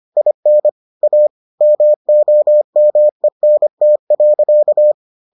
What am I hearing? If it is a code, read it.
Morse code, heard as a keyed tone, about 25 wpm, 600 Hz. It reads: IN A MOMENT.